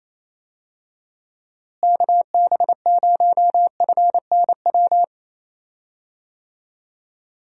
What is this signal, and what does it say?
Morse code, heard as a keyed tone, about 28 wpm, 700 Hz. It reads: KB0FNW
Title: USA Call Signs - 6 Characters - Repeated 2x - 28wpm